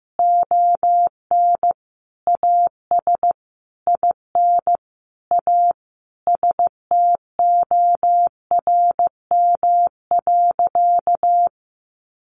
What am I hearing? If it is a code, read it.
Morse code, heard as a keyed tone, about 15 wpm, 700 Hz. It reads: ON AS IN A STORM.
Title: Sentences from Top 1500 Words 15wpm